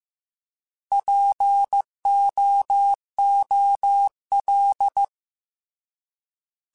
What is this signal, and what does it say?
Morse code, heard as a keyed tone, about 15 wpm, 800 Hz. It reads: POOL
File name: Alphabetical / Alphabetical mp3s / Ps @13